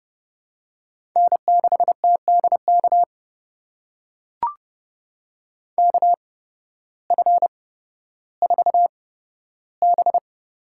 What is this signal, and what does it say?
Morse code, heard as a keyed tone, about 30 wpm, spaced as for 14 wpm, 700 Hz. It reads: N6TDK KF4B
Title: USA Call Signs - 8x Character Spacing - 30wpm